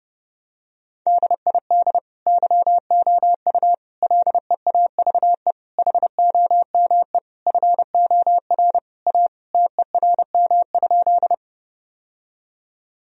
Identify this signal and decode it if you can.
Morse code, heard as a keyed tone, about 30 wpm, 700 Hz. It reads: DID YOU LEAVE HOME FOR A TERM?
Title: Sentences from Top 1000 Words - Review Part 4 - 30wpm